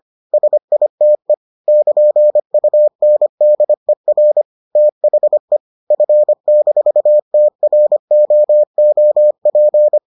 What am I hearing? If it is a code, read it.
Morse code, heard as a keyed tone, about 25 wpm, 600 Hz. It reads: SITE <KN>UNDER THE F-TROOP